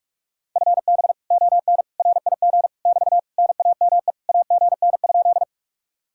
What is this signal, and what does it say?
Morse code, heard as a keyed tone, about 44 wpm, 700 Hz. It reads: FB ON RIG <BT> NAME AGN?